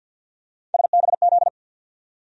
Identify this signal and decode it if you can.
Morse code, heard as a keyed tone, about 53 wpm, 700 Hz. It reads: SBZ
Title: USA Call Sign Postfixes - 3 Letters 50wpm